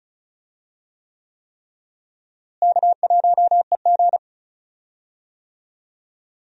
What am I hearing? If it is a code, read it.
Morse code, heard as a keyed tone, about 34 wpm, 700 Hz. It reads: K1EG
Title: USA Call Signs - 4 Characters - Repeated 3x - 35wpm